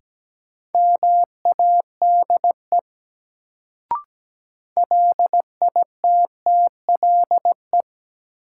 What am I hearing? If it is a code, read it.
Morse code, heard as a keyed tone, about 17 wpm, 700 Hz. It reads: MADE LITTLE
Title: Sentences from Top 200 Words - New Word Drill - 17wpm